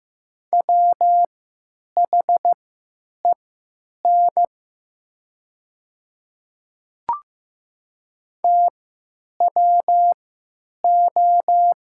Morse code, heard as a keyed tone, about 15 wpm, spaced as for 9 wpm, 700 Hz. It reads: WHEN TWO
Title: 3x Character Spacing - Top 100 Words - 15wpm